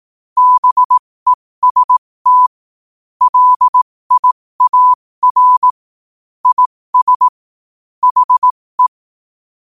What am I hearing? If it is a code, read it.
Morse code, heard as a keyed tone, about 18 wpm, spaced as for 15 wpm, 1000 Hz. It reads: BEST LIAR IS HE
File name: qotdfiles / 15wpm